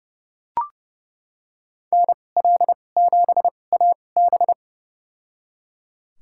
Morse code, heard as a keyed tone, about 32 wpm, spaced as for 22 wpm, 700 Hz. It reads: NL7AB
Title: USA Call Signs - 5 Characters - 2x Character Spacing - 30wpm